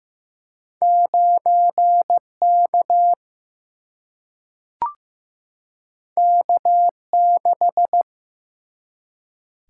Morse code, heard as a keyed tone, about 15 wpm, 700 Hz. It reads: 9K K6
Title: Single-Letter Single-Number 15wpm